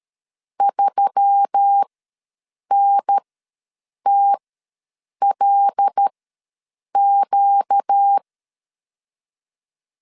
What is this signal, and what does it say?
Morse code, heard as a keyed tone, about 13 wpm, spaced as for 7 wpm, 800 Hz. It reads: 3NTLQ
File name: LEZIONI / 6 PKX0 / 6 P ws 12wpm 3117